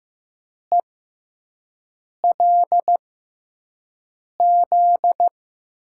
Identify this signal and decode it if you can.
Morse code, heard as a keyed tone, about 15 wpm, spaced as for 5 wpm, 700 Hz. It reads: ELZ
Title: USA Call Signs - 6 Characters - 6x Character Spacing - 15wpm